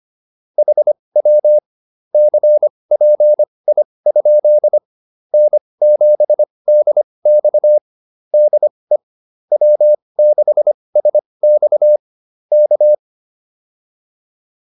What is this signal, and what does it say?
Morse code, heard as a keyed tone, about 25 wpm, spaced as for 20 wpm, 600 Hz. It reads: HW CPI? N7DX DE W6SX K
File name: QSO102 20